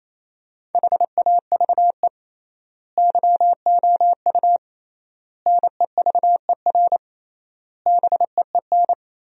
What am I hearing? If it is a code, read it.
Morse code, heard as a keyed tone, about 28 wpm, 700 Hz. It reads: HAVE YOU NEVER BEEN
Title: 3x Word-Spacing - Sentences from Top 300 Words - 28wpm